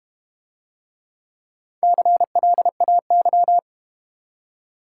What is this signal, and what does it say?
Morse code, heard as a keyed tone, about 32 wpm, 690 Hz. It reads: CLAY